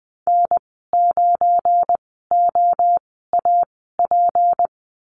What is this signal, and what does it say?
Morse code, heard as a keyed tone, about 20 wpm, spaced as for 15 wpm, 700 Hz. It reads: N9OAP
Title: 193 Lesson 31g - Letter v - Call Signs 20wpm